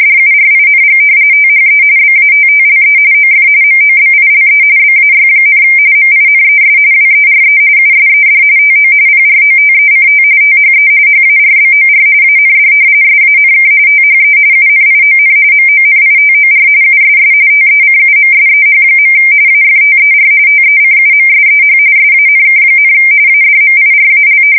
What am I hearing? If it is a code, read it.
RTTY, about 45.45 baud. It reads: DURING THE BATTLE OF MOBILE BAY, MINES WERE USED TO PROTECT THE HARBOR AND SANK THE UNION MONITOR USS TECUMSEH. AFTER TECUMSEH SANK, ADMIRAL
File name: fsk170 / usnavy3